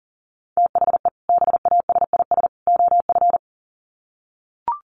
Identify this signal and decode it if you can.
Morse code, heard as a keyed tone, about 40 wpm, 700 Hz. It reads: THE BASIS OF
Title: Sets of 3 Words 40wpm